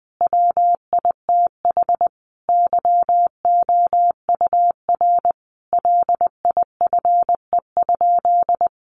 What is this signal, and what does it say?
Morse code, heard as a keyed tone, about 20 wpm, 700 Hz. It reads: WITH YOUR LIFE?
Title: Sentences from Top 900 Words 20wpm